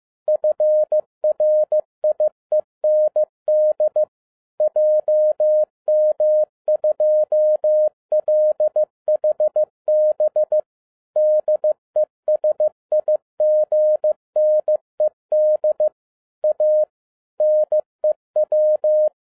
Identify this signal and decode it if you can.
Morse code, heard as a keyed tone, about 15 wpm, 610 Hz. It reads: FRIEND JM2LHB DESIGNED A NEW
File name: 2012-01-mp3 / n13-A NEWLY DESIGNED QSL CARD-15wpm